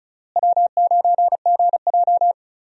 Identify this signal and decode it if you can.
Morse code, heard as a keyed tone, about 36 wpm, 700 Hz. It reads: W9GJ
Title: USA Call Signs - 4 Characters - Repeated 3x - 35wpm